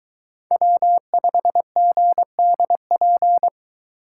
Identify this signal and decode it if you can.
Morse code, heard as a keyed tone, about 22 wpm, 700 Hz. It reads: W5GDP